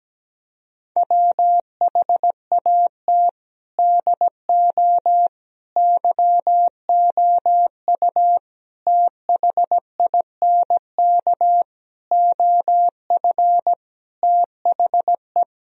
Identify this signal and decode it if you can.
Morse code, heard as a keyed tone, about 17 wpm, 700 Hz. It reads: WHAT DO YOU THINK OF THE
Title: Sentences from Top 1000 Words - Review Part 4 - 17wpm